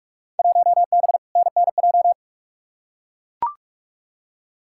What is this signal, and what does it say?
Morse code, heard as a keyed tone, about 43 wpm, 700 Hz. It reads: 1B NNJ
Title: Field Day Exchanges - 45wpm